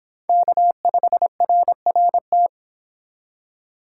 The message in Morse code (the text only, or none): K5RRT